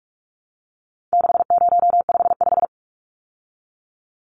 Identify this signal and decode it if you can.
Morse code, heard as a keyed tone, about 44 wpm, 700 Hz. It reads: 6055
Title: Numbers - 4-Digit 45wpm